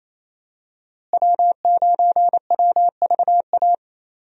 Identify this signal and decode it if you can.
Morse code, heard as a keyed tone, about 28 wpm, 700 Hz. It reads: W9WVA
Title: USA Call Signs - 5 Characters - Repeated 2x - 28wpm